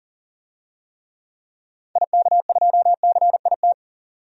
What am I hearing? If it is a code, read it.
Morse code, heard as a keyed tone, about 40 wpm, 700 Hz. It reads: IK2CIT
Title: DXCC-100 Call Signs - 6 Characters - 40wpm